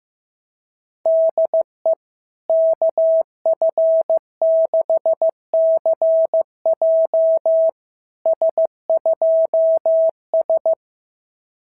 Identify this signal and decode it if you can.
Morse code, heard as a keyed tone, about 15 wpm, 660 Hz. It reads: DE KF6CJ S2S
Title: SOTA QSO Practice 15wpm